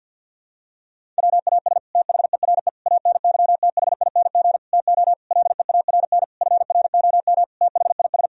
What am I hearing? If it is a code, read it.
Morse code, heard as a keyed tone, about 49 wpm, 700 Hz. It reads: WAS THERE ANYTHING TO LEARN FROM THIS
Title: Sentences from Top 1000 Words - Review Part 1 - 50wpm